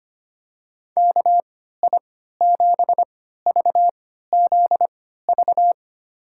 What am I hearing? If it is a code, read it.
Morse code, heard as a keyed tone, about 25 wpm, spaced as for 14 wpm, 700 Hz. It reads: KI7VZV
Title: USA Call Signs - 6 Characters - 3x Character Spacing - 25wpm